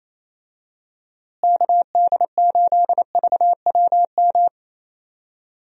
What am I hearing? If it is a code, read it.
Morse code, heard as a keyed tone, about 28 wpm, 700 Hz. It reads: KD8VWM
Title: USA Call Signs - 6 Characters - Repeated 2x - 28wpm